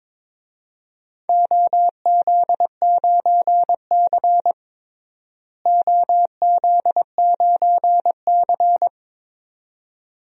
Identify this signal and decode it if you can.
Morse code, heard as a keyed tone, about 22 wpm, 700 Hz. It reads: OZ9C OZ9C
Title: DXCC-100 Call Signs - 4 Characters - Repeated 2x - 22wpm